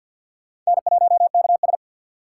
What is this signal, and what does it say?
Morse code, heard as a keyed tone, about 52 wpm, 700 Hz. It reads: N1KS
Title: USA Call Signs - 4 Characters - 2x Character Spacing - 50wpm